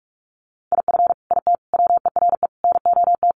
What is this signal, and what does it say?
Morse code, heard as a keyed tone, about 44 wpm, 700 Hz. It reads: IF IT WERE NOT
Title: Sets of 5 Words 45wpm